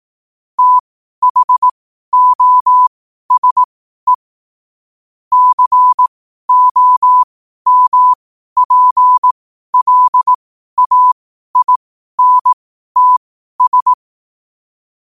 Morse code, heard as a keyed tone, about 18 wpm, spaced as for 12 wpm, 1000 Hz. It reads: THOSE COMPLAINTS
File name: qotdfiles / 12wpm